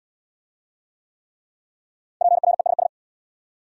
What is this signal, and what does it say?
Morse code, heard as a keyed tone, about 64 wpm, 700 Hz. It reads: PRESS